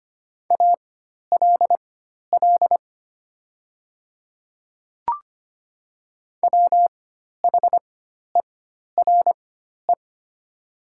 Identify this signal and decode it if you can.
Morse code, heard as a keyed tone, about 25 wpm, spaced as for 12 wpm, 700 Hz. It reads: ALL WHERE
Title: 4x Character Spacing - Top 200 Words - 25wpm